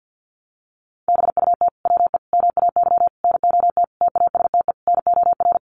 Morse code, heard as a keyed tone, about 49 wpm, 700 Hz. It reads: BUT WE MAY NOT TASTE NOR
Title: Sentences from Top 1500 Words 50wpm